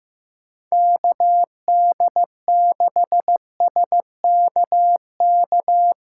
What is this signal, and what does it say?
Morse code, heard as a keyed tone, about 15 wpm, 700 Hz. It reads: KD6SKK
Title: USA Callsign DE Callsign 15wpm